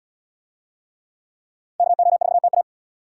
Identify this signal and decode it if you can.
Morse code, heard as a keyed tone, about 77 wpm, 700 Hz. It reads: KO4TA